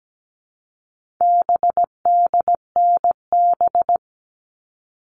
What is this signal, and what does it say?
Morse code, heard as a keyed tone, about 17 wpm, 700 Hz. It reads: BDNB
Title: All Cut Numbers - 4-Digit 17wpm